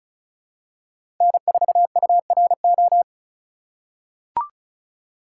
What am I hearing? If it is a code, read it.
Morse code, heard as a keyed tone, about 34 wpm, 700 Hz. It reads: N4URO
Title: USA Call Signs - 5 Characters - 3x Character Spacing - 35wpm